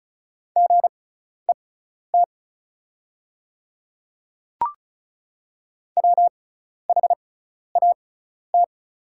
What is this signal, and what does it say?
Morse code, heard as a keyed tone, about 36 wpm, spaced as for 12 wpm, 700 Hz. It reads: GET WHAT
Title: 6x Character Spacing - Top 200 Words - 35wpm